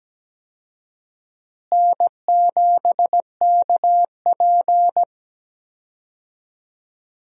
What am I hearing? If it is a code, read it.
Morse code, heard as a keyed tone, about 17 wpm, 700 Hz. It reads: N7KP
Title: USA Call Signs - 4 Characters - Repeated 2x - 17wpm